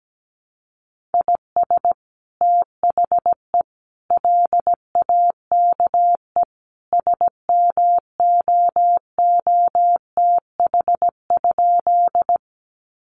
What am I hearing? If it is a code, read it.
Morse code, heard as a keyed tone, about 17 wpm, 700 Hz. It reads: IS THE LAKE SMOOTH?
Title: Sentences from Top 1400 Words 17wpm